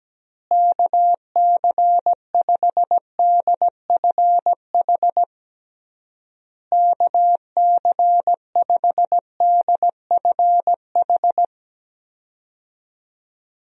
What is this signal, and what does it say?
Morse code, heard as a keyed tone, about 17 wpm, 700 Hz. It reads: KC5DFH KC5DFH
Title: USA Call Signs - 6 Characters - Repeated 3x - 17wpm